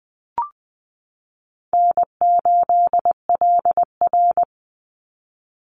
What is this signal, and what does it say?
Morse code, heard as a keyed tone, about 20 wpm, 700 Hz. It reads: N8LR
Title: Easy Call Signs Repeated 2x - 20wpm